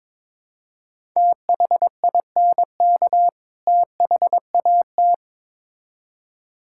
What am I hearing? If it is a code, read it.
Morse code, heard as a keyed tone, about 22 wpm, 700 Hz. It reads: THINK THAT
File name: Build A Phrase - 2-Word Phrases - 22wpm